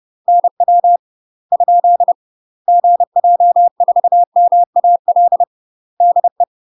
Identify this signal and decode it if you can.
Morse code, heard as a keyed tone, about 30 wpm, 700 Hz. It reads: NW ? GJ4MAL DE
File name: UK MASTER QSOs TRAINING / 30WPM / UK MASTER QSOs 0000